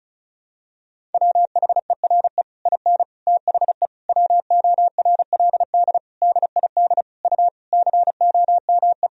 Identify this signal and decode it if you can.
Morse code, heard as a keyed tone, about 34 wpm, 700 Hz. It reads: WHERE IN THE WORLD DID U COME
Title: Sentences from Top 200 Words and Abbreviations 35wpm